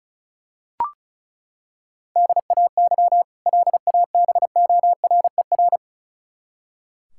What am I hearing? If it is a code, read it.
Morse code, heard as a keyed tone, about 34 wpm, 700 Hz. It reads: DAY LABORER
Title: K1USN SST QSO Practice 35wpm